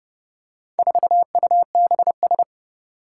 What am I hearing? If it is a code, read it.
Morse code, heard as a keyed tone, about 30 wpm, 700 Hz. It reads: 4UBS
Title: Four Challenging Letter-Number Combinations 30wpm